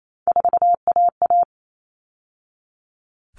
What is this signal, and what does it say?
Morse code, heard as a keyed tone, about 28 wpm, 700 Hz. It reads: 4AA
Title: All Cut Numbers - 3-Digit 28wpm